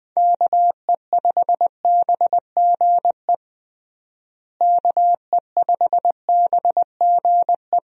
Morse code, heard as a keyed tone, about 20 wpm, 700 Hz. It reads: KE5BGE KE5BGE
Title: USA Call Signs - 6 Characters - Repeated 2x - 20wpm